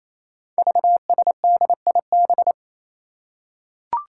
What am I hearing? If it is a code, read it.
Morse code, heard as a keyed tone, about 27 wpm, 700 Hz. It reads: VSDIB
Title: Five Challenging Letter-Number Combinations 28wpm